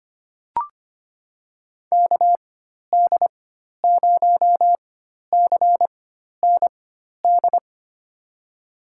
Morse code, heard as a keyed tone, about 25 wpm, spaced as for 12 wpm, 700 Hz. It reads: KD0CND
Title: USA Call Signs - 4x Character Spacing - 25wpm